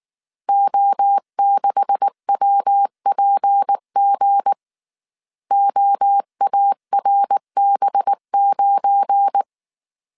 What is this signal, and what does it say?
Morse code, heard as a keyed tone, about 19 wpm, 800 Hz. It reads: O6WPG OARB9
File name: CW / DET / 1 9 1500C S A 18wpm 2459